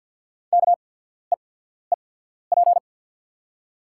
Morse code, heard as a keyed tone, about 48 wpm, spaced as for 14 wpm, 690 Hz. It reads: KEEP
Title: 8x Character Spacing - Top 300 Words - 50wpm